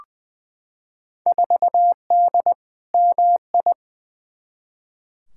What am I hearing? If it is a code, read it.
Morse code, heard as a keyed tone, about 20 wpm, 700 Hz. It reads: 4D MI
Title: Field Day Exchanges Encore - 20wpm